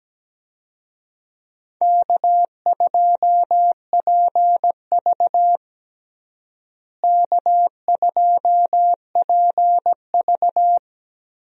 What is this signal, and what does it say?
Morse code, heard as a keyed tone, about 17 wpm, 700 Hz. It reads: K2PV K2PV
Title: USA Call Signs - 4 Characters - Repeated 3x - 17wpm